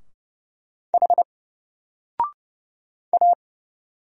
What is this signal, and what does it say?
Morse code, heard as a keyed tone, about 29 wpm, 700 Hz. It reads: H A
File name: Single-Letter-Number-30wpm